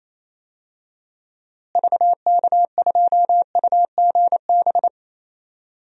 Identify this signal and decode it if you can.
Morse code, heard as a keyed tone, about 28 wpm, 700 Hz. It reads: VK2UGB
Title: DXCC-100 Call Signs - 6 Characters - Repeated 3x - 28wpm